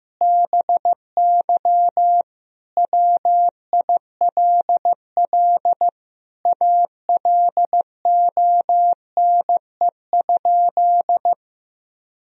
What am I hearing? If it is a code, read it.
Morse code, heard as a keyed tone, about 16 wpm, 700 Hz. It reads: BY WILL ALONE?
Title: Sentences from Top 1000 Words - Review Part 2 - 15wpm